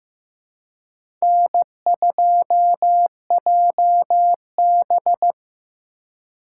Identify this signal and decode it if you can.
Morse code, heard as a keyed tone, about 15 wpm, 700 Hz. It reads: N2JB
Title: USA Call Signs - 4 Characters - 2x Character Spacing - 15wpm